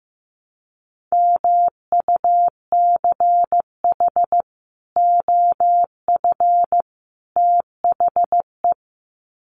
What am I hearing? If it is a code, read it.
Morse code, heard as a keyed tone, about 15 wpm, 700 Hz. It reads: MUCH OF THE
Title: Sets of 3 Words 15wpm